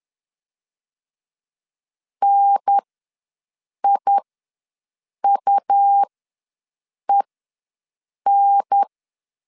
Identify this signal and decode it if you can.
Morse code, heard as a keyed tone, about 10 wpm, spaced as for 6 wpm, 800 Hz. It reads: NIUEN
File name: LEZIONI / 5 LBVC / 5 L ws 10wpm 3636